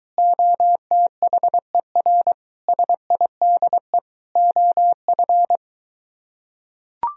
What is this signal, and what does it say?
Morse code, heard as a keyed tone, about 24 wpm, 700 Hz. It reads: OTHER SIDE OF